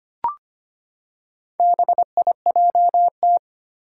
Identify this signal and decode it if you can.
Morse code, heard as a keyed tone, about 25 wpm, 700 Hz. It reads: BIJT
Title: Four Challenging Letter-Number Combinations 25wpm